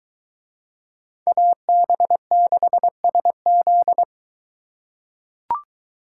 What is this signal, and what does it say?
Morse code, heard as a keyed tone, about 21 wpm, 700 Hz. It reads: AB6SZ